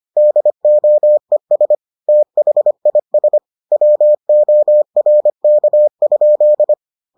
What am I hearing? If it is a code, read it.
Morse code, heard as a keyed tone, about 25 wpm, 600 Hz. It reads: DOES THIS WORK?